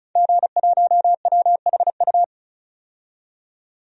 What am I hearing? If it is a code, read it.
Morse code, heard as a keyed tone, about 36 wpm, 700 Hz. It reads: G1WHU
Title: DXCC-100 Call Signs - 5 Characters - 35wpm